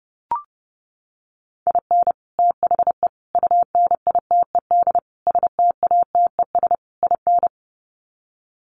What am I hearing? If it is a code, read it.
Morse code, heard as a keyed tone, about 30 wpm, 700 Hz. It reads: IN THE UNITED STATES IN
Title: Sets of 5 Words 30wpm